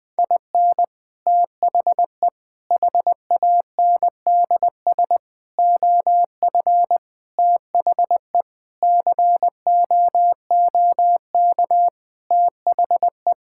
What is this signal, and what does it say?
Morse code, heard as a keyed tone, about 21 wpm, 700 Hz. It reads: IN THE HANDS OF THE COOK THE
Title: Sentences with 5-Word Phrases - 20wpm